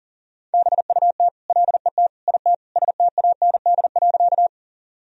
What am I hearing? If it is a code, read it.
Morse code, heard as a keyed tone, about 40 wpm, 700 Hz. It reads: BUT LET IT STAND.
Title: Sentences from Top 1000 Words - Review Part 1 - 40wpm